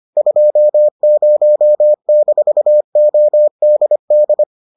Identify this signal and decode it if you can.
Morse code, heard as a keyed tone, about 25 wpm, 600 Hz. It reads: 20-ODD